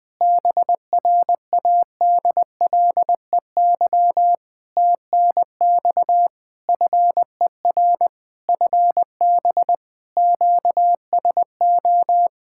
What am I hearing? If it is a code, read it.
Morse code, heard as a keyed tone, about 20 wpm, 700 Hz. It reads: BRADLEY TNX FER FB QSO
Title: Practice QSO - Exchange 4 - Ending - 20wpm